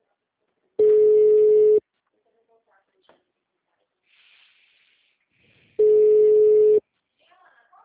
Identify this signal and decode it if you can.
Ringback tone (Europe, 425 Hz), with speech.
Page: Campinas - SP